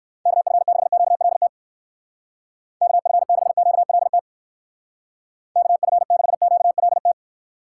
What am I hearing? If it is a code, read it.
Morse code, heard as a keyed tone, about 64 wpm, 700 Hz. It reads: KF6QLT KF6QLT KF6QLT